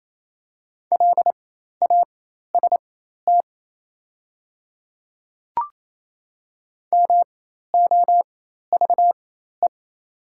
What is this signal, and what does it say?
Morse code, heard as a keyed tone, about 28 wpm, spaced as for 13 wpm, 700 Hz. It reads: LAST MOVE
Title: 4x Character Spacing - Top 300 Words - 28wpm